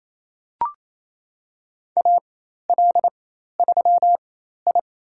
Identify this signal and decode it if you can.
Morse code, heard as a keyed tone, about 28 wpm, spaced as for 13 wpm, 700 Hz. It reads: AL3I
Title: USA Call Signs - 4 Characters - 4x Character Spacing - 28wpm